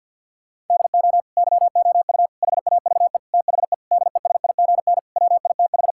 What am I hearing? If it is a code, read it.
Morse code, heard as a keyed tone, about 52 wpm, 700 Hz. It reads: DO YOU HAVE THE DESIGN WITH
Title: Sentences from Top 1000 Words - Review Part 5 - 50wpm